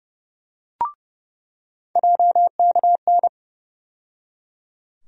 Morse code, heard as a keyed tone, about 31 wpm, 700 Hz. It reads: JKN
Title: USA Call Sign Postfixes - 3 Letters 30wpm